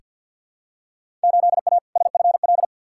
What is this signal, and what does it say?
Morse code, heard as a keyed tone, about 49 wpm, 700 Hz. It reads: 8A SFL
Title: Field Day Exchanges Encore - 50wpm